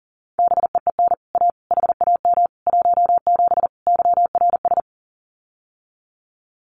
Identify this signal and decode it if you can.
Morse code, heard as a keyed tone, about 40 wpm, 700 Hz. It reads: BEEN A HAM 17 YRS
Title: Common QSO Phrases Encore 40wpm